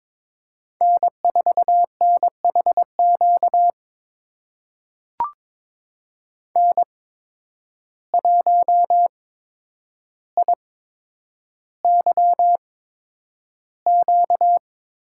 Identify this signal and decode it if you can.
Morse code, heard as a keyed tone, about 22 wpm, spaced as for 9 wpm, 700 Hz. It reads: N4NHQ N1IYQ
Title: USA Call Signs - 5 Characters - 8x Character Spacing - 22wpm